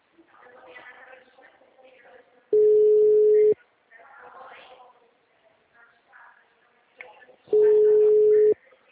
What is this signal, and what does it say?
Ringback tone (Europe, 425 Hz), with speech.
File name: out- -2048- - - .31781